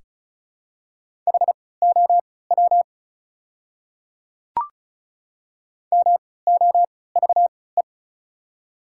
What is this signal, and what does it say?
Morse code, heard as a keyed tone, about 34 wpm, spaced as for 20 wpm, 690 Hz. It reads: HOW MOVE
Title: 3x Character Spacing - Top 300 Words - 35wpm